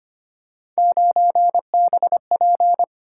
Morse code, heard as a keyed tone, about 31 wpm, 700 Hz. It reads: 9BP
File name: Speed-Racing - Three Letter-Number Combinations - 25wpm